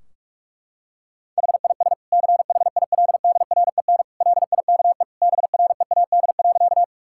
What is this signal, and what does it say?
Morse code, heard as a keyed tone, about 46 wpm, 700 Hz. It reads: HIS CHILDREN LIKE BREAD.
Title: Sentences from Top 1600 Words 45wpm